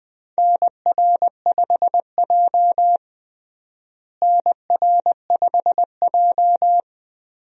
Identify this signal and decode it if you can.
Morse code, heard as a keyed tone, about 21 wpm, 700 Hz. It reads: NR5J NR5J
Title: USA Call Signs - 4 Characters - Repeated 2x - 20wpm